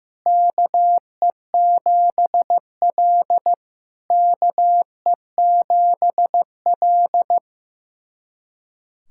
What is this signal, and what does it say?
Morse code, heard as a keyed tone, about 15 wpm, 700 Hz. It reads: KE7L KE7L
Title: SOTA QSO Practice 15wpm